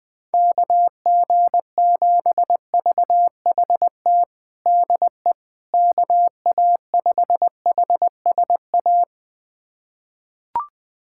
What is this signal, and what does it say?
Morse code, heard as a keyed tone, about 20 wpm, 700 Hz. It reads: KG7VHT DE KA5HSA
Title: USA Callsign DE Callsign - 6 Characters - 20wpm